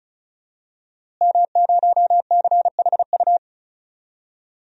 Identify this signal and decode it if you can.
Morse code, heard as a keyed tone, about 34 wpm, 700 Hz. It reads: M0CHU
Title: DXCC-100 Call Signs - 5 Characters - Repeated 2x - 35wpm